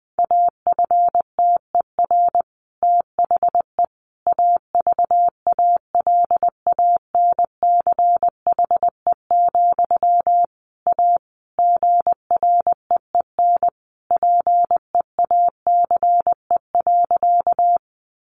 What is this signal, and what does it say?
Morse code, heard as a keyed tone, about 19 wpm, 700 Hz. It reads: AFTER THE AVALANCHE, A GREEN PEACE.
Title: A Sound Of Thunder 20wpm